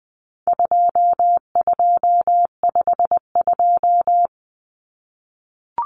Morse code, heard as a keyed tone, about 20 wpm, 700 Hz. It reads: 2252
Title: Common Cut Numbers - 4-Digit 20wpm